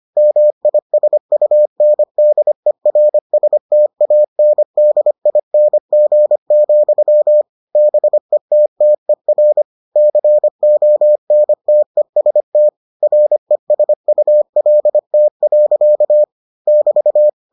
Morse code, heard as a keyed tone, about 25 wpm, 600 Hz. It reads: MISUNDERSTANDING, BETTER CONTEST RESULT. <BT>